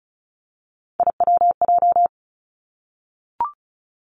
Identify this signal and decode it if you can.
Morse code, heard as a keyed tone, about 36 wpm, 700 Hz. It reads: IWJ
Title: Three Challenging Letter-Number Combinations 35wpm